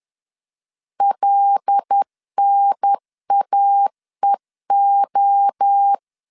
Morse code, heard as a keyed tone, about 10 wpm, 800 Hz. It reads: LNAEO
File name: LEZIONI / 7 GF9Z / 7 G 10wpm 2501